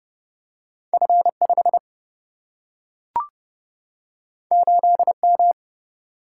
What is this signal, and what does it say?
Morse code, heard as a keyed tone, about 30 wpm, 700 Hz. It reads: F5 8M
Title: Single-Letter Single-Number 30wpm